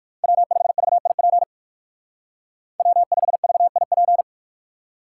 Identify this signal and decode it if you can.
Morse code, heard as a keyed tone, about 59 wpm, 700 Hz. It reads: W5VIP W5VIP